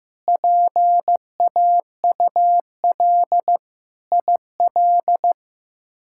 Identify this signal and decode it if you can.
Morse code, heard as a keyed tone, about 15 wpm, 700 Hz. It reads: PAUL IL
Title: K1USN SST QSO Practice - 15wpm